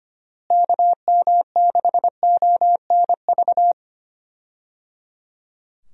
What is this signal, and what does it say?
Morse code, heard as a keyed tone, about 25 wpm, 700 Hz. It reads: KM6ONV
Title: USA Call Signs - 6 Characters - Repeated 3x - 25wpm